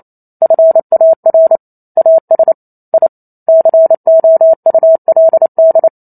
Morse code, heard as a keyed tone, about 28 wpm, 650 Hz. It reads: FAR AS I COULD